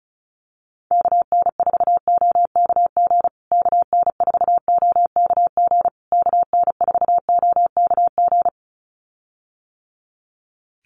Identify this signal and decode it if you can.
Morse code, heard as a keyed tone, about 34 wpm, 700 Hz. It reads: KN4OKG KN4OKG KN4OKG
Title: USA Call Signs Repeated 3x 35wpm